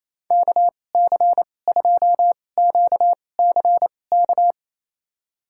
Morse code, heard as a keyed tone, about 28 wpm, spaced as for 20 wpm, 700 Hz. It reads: KC2QCK
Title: USA Call Signs - 6 Characters - 2x Character Spacing - 28wpm